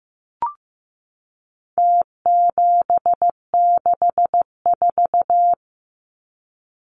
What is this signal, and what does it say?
Morse code, heard as a keyed tone, about 15 wpm, 700 Hz. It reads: T764
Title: Numbers - Contest Style - 4-Digit 15wpm